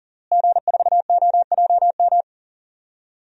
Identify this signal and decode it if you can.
Morse code, heard as a keyed tone, about 40 wpm, 700 Hz. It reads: G4OJM
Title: DXCC-100 Call Signs - 5 Characters - Repeated 3x - 40wpm